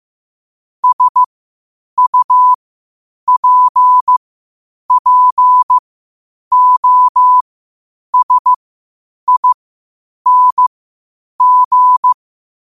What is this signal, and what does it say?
Morse code, heard as a keyed tone, about 15 wpm, spaced as for 8 wpm, 1000 Hz. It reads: SUPPOSING
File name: qotdfiles / 8wpm